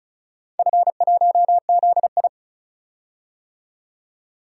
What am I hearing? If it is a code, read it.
Morse code, heard as a keyed tone, about 36 wpm, 690 Hz. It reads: F1ZI
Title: DXCC-100 Call Signs - 4 Characters - 35wpm